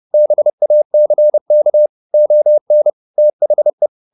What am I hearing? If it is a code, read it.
Morse code, heard as a keyed tone, about 30 wpm, 600 Hz. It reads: BACK ON THE
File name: the old man and the sea 2 30wpm